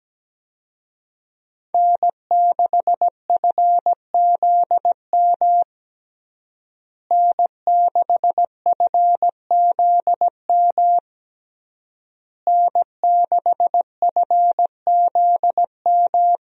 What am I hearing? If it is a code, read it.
Morse code, heard as a keyed tone, about 17 wpm, 700 Hz. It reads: N6FZM N6FZM N6FZM
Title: USA Call Signs - 5 Characters - Repeated 3x - 17wpm